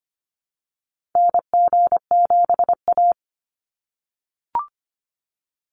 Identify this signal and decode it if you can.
Morse code, heard as a keyed tone, about 25 wpm, 690 Hz. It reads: NG7A
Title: NA SOTA Activator Callsigns 25wpm